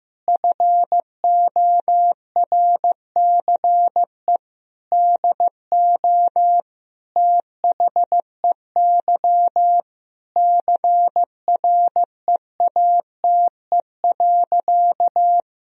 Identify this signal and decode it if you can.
Morse code, heard as a keyed tone, about 15 wpm, 700 Hz. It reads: FORCE DO THEY CREATE.
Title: Sentences from Top 1000 Words - Review Part 3 - 15wpm